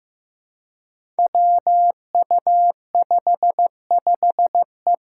Morse code, heard as a keyed tone, about 15 wpm, 700 Hz. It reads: WU55E
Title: Five Challenging Letter-Number Combinations 15wpm